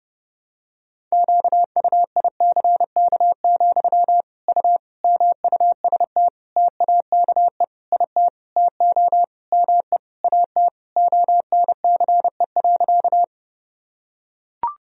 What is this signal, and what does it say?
Morse code, heard as a keyed tone, about 30 wpm, 700 Hz. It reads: QUICK, U MUST TAKE IT TO ME AT ONCE.
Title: Sentences from Top 500 Words and Abbreviations 30wpm